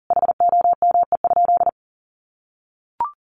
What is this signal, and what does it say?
Morse code, heard as a keyed tone, about 37 wpm, 700 Hz. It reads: HOME?
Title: Sentences from Top 1400 Words 40wpm